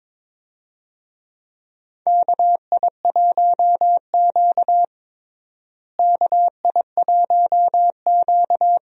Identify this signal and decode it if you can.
Morse code, heard as a keyed tone, about 22 wpm, 700 Hz. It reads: KI1Q KI1Q
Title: USA Call Signs - 4 Characters - Repeated 2x - 22wpm